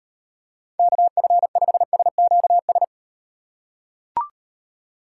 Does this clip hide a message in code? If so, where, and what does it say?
Yes: Morse code from 0.8 to 2.8 s, heard as a keyed tone, about 38 wpm, 700 Hz. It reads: KF5SQS